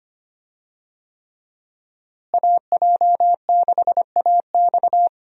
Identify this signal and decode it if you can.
Morse code, heard as a keyed tone, about 25 wpm, 700 Hz. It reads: AJ6AX
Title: USA Call Signs - 5 Characters - Repeated 3x - 25wpm